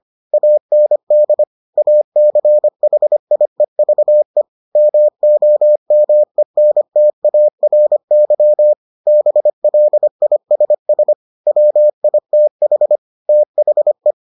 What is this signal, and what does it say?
Morse code, heard as a keyed tone, about 25 wpm, 600 Hz. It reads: AND ACHIEVE MOMENTARY BLISS WITH THE